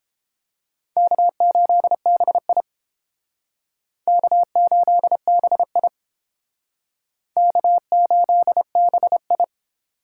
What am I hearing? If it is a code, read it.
Morse code, heard as a keyed tone, about 30 wpm, 700 Hz. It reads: K8BI K8BI K8BI